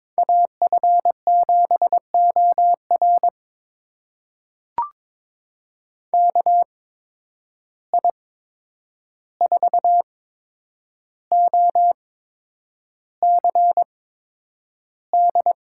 Morse code, heard as a keyed tone, about 22 wpm, spaced as for 9 wpm, 700 Hz. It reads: AF7OR KI4OCD
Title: USA Call Signs - 8x Character Spacing - 22wpm